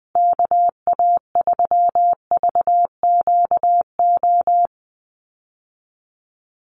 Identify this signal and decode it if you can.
Morse code, heard as a keyed tone, about 20 wpm, 700 Hz. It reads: KA3VQO
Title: USA Call Signs Repeated 2x 20wpm